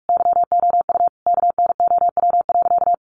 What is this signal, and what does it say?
Morse code, heard as a keyed tone, about 44 wpm, 700 Hz. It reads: YOU KNOW.
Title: Sentences from Top 100 Words 45wpm